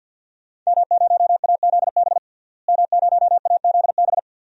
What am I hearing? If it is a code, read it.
Morse code, heard as a keyed tone, about 47 wpm, 700 Hz. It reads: M0AZB M0AZB
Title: DXCC-100 Call Signs - 5 Characters - Repeated 2x - 50wpm